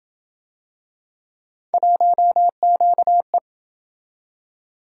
Morse code, heard as a keyed tone, about 28 wpm, 700 Hz. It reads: 1QE